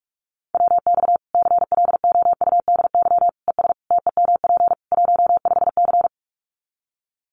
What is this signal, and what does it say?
Morse code, heard as a keyed tone, about 44 wpm, 700 Hz. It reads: WX CLOUDY ES TEMP 15C
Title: Common QSO Phrases 45wpm